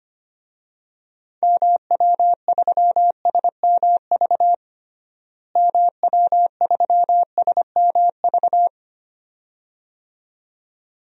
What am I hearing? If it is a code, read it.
Morse code, heard as a keyed tone, about 25 wpm, 700 Hz. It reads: MW3SMV MW3SMV
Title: DXCC-100 Call Signs - 6 Characters - Repeated 2x - 25wpm